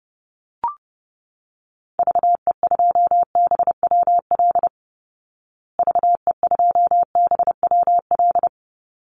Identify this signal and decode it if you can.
Morse code, heard as a keyed tone, about 30 wpm, 700 Hz. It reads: VE2BWL VE2BWL
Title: Call Signs Repeated 3x - 30wpm